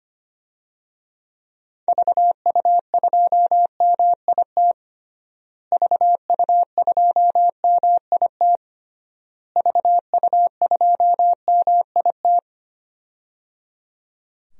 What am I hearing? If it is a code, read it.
Morse code, heard as a keyed tone, about 25 wpm, 700 Hz. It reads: VU2MIT VU2MIT VU2MIT
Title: DXCC-100 Call Signs - 6 Characters - Repeated 3x - 25wpm